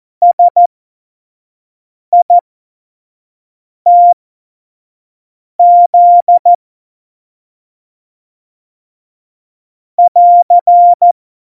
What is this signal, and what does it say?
Morse code, heard as a keyed tone, about 14 wpm, spaced as for 5 wpm, 700 Hz. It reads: SITZ <AR>
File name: Morsekurs / Witze / Witz 28. August 2018, 25 F70